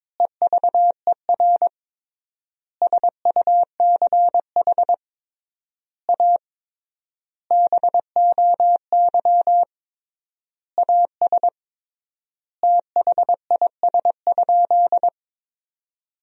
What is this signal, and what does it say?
Morse code, heard as a keyed tone, about 22 wpm, 700 Hz. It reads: EVER SUCH A BOY AS THIS?
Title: 3x Word-Spacing - Sentences from Top 400 Words - 22wpm